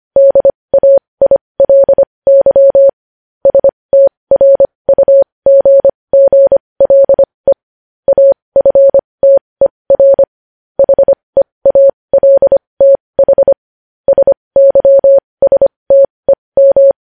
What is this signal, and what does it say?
Morse code, heard as a keyed tone, about 25 wpm, spaced as for 20 wpm, 550 Hz. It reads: DAILY STRUGGLE AFTER HEALTH SYSTEM